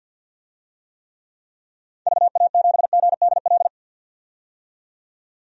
Morse code, heard as a keyed tone, about 46 wpm, 700 Hz. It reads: VA7GDL
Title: DXCC-100 Call Signs - 6 Characters - 50wpm